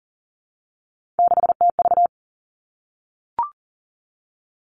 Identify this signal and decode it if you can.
Morse code, heard as a keyed tone, about 39 wpm, 700 Hz. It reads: 6TV
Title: Three Challenging Letter-Number Combinations 40wpm